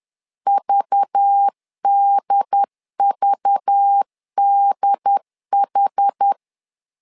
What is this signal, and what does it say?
Morse code, heard as a keyed tone, about 10 wpm, 800 Hz. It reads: VDVDH
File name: LEZIONI / 5 LBVC / 5 LBV 10wpm 2248